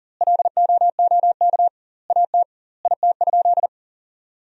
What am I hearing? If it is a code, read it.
Morse code, heard as a keyed tone, about 37 wpm, 700 Hz. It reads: LOOK AT IT?
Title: 2x Word-Spacing - Sentences from Top 100 Words - 40wpm